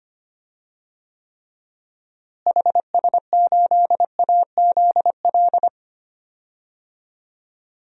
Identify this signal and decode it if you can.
Morse code, heard as a keyed tone, about 25 wpm, 700 Hz. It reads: HS8AZL
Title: DXCC-100 Call Signs - 6 Characters - Repeated 2x - 25wpm